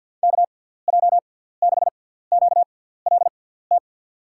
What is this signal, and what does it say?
Morse code, heard as a keyed tone, about 53 wpm, spaced as for 17 wpm, 700 Hz. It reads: KJ6QLT
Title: USA Call Signs - 6 Characters - 6x Character Spacing - 50wpm